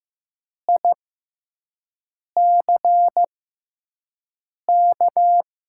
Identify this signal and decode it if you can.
Morse code, heard as a keyed tone, about 15 wpm, spaced as for 5 wpm, 700 Hz. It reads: ICK
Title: 6x Character Spacing - Top 500 Words - 15wpm